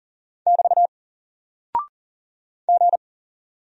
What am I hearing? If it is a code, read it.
Morse code, heard as a keyed tone, about 40 wpm, 700 Hz. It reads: <BT> G
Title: Single Character (Letter:Number:Punctuation:Prosign) Alt 40wpm